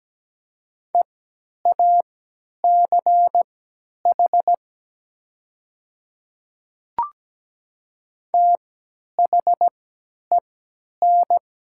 Morse code, heard as a keyed tone, about 17 wpm, spaced as for 10 wpm, 700 Hz. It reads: EACH THEN
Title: 3x Character Spacing - Top 100 Words - 17wpm